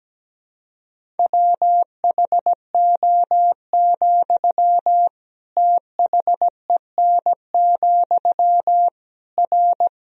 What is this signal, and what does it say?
Morse code, heard as a keyed tone, about 17 wpm, 700 Hz. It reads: WHO, THEN, R
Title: Sentences from Top 100 Words and Abbreviations 17wpm